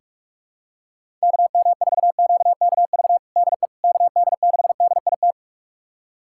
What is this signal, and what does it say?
Morse code, heard as a keyed tone, about 41 wpm, 700 Hz. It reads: KM4QKV DE KD6DIT
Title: USA Callsign DE Callsign - 6 Characters - 45wpm